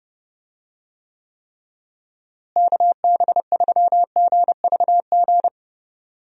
Morse code, heard as a keyed tone, about 30 wpm, 700 Hz. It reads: KB3GVG
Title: USA Call Signs - 6 Characters - Repeated 3x - 30wpm